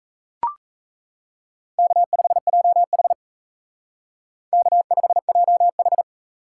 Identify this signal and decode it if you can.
Morse code, heard as a keyed tone, about 35 wpm, 700 Hz. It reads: K5JH K5JH
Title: Speed Racing - USA 1x2 Call Signs 28wpm